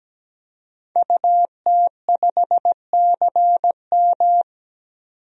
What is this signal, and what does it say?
Morse code, heard as a keyed tone, about 17 wpm, 700 Hz. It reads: UT5CM
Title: DXCC-100 Call Signs - 5 Characters - 17wpm